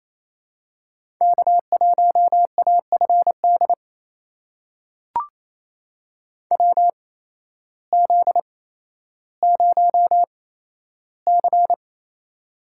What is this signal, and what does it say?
Morse code, heard as a keyed tone, about 28 wpm, spaced as for 13 wpm, 700 Hz. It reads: K1AFD WZ0C
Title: USA Call Signs - 8x Character Spacing - 28wpm